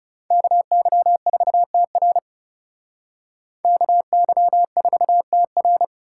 Morse code, heard as a keyed tone, about 40 wpm, 700 Hz. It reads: KY4TR KY4TR